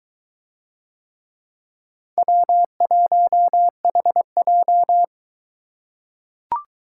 Morse code, heard as a keyed tone, about 22 wpm, 700 Hz. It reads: W1HJ